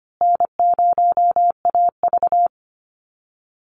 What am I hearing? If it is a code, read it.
Morse code, heard as a keyed tone, about 25 wpm, 700 Hz. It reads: N0AV
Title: Easy Call Signs Repeated 3x - 25wpm